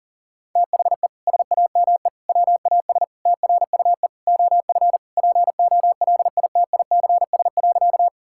Morse code, heard as a keyed tone, about 40 wpm, 700 Hz. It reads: THE SAME WAS TRUE OF POLITICS.
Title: Sentences from Top 1000 Words - Review Part 6 - 40wpm